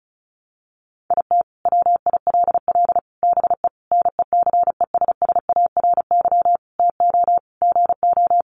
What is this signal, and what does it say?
Morse code, heard as a keyed tone, about 34 wpm, 700 Hz. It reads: IT WILL BE NECESSARY TO GO
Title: Sentences from Top 800 Words 35wpm